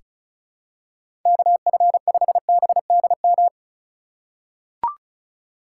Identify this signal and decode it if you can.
Morse code, heard as a keyed tone, about 36 wpm, 700 Hz. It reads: KF5BDM
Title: USA Call Signs - 6 Characters - 3x Character Spacing - 35wpm